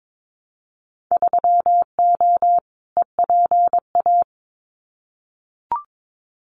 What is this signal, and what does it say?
Morse code, heard as a keyed tone, about 22 wpm, 700 Hz. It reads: 3O EPA
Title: Winter Field Day Exchanges Encore 22wpm